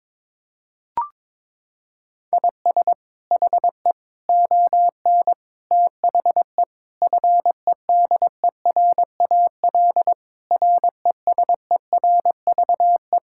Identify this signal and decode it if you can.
Morse code, heard as a keyed tone, about 22 wpm, 700 Hz. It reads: IS HE ON THE FEDERAL RESERVE
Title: Sentences from Top 1000 Words - Review Part 6 - 22wpm